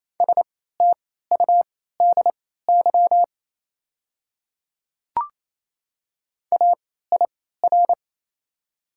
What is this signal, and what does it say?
Morse code, heard as a keyed tone, about 28 wpm, spaced as for 16 wpm, 700 Hz. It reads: STUDY AIR
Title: 3x Character Spacing - Top 300 Words - 28wpm